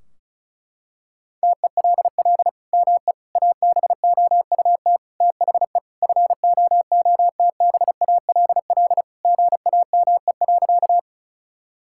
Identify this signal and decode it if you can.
Morse code, heard as a keyed tone, about 34 wpm, 700 Hz. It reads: TELL ME ABOUT THE FOOTBALL GAME.
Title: Sentences from Top 1600 Words 35wpm